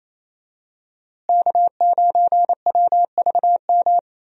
Morse code, heard as a keyed tone, about 28 wpm, 700 Hz. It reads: K9WVM
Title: USA Call Signs - 5 Characters - Repeated 3x - 28wpm